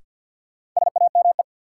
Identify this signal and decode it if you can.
Morse code, heard as a keyed tone, about 53 wpm, 700 Hz. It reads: SAME
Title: Sets of 2 Words - New Word Drill - 50wpm